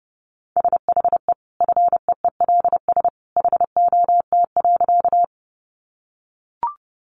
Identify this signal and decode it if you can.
Morse code, heard as a keyed tone, about 30 wpm, 700 Hz. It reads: SHE FEELS HOT.
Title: Additional Easy Phrases 30wpm